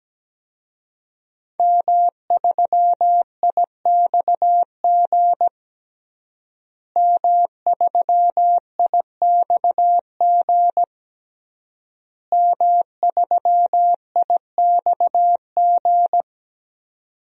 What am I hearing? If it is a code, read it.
Morse code, heard as a keyed tone, about 17 wpm, 700 Hz. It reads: M3IXG M3IXG M3IXG
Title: DXCC-100 Call Signs - 5 Characters - Repeated 3x - 17wpm